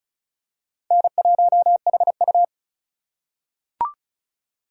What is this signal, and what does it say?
Morse code, heard as a keyed tone, about 36 wpm, 700 Hz. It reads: N1HU
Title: USA Call Signs - 4 Characters - 3x Character Spacing - 35wpm